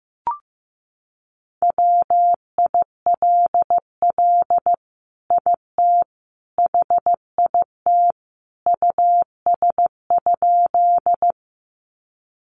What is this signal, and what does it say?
Morse code, heard as a keyed tone, about 15 wpm, 700 Hz. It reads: WILL IT HIT US?
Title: Sentences from Top 1200 Words 15wpm